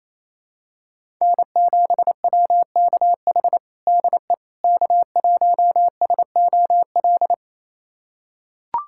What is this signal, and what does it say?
Morse code, heard as a keyed tone, about 28 wpm, 700 Hz. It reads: N7WKH DE K1SOL
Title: USA Callsign DE Callsign - 5 Characters - 28wpm